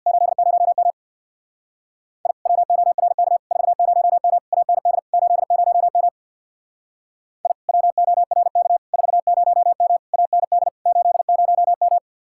Morse code, heard as a keyed tone, about 66 wpm, 700 Hz. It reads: 80M I WORK 40M AND 80M I WORK 40M AND 80M